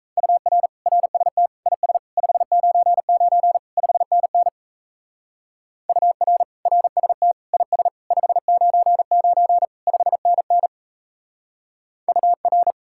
Morse code, heard as a keyed tone, about 34 wpm, 700 Hz. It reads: UR RST IS 599 5NN UR RST IS 599 5NN UR
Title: Speed-Racing Common QSO Phrases - 28wpm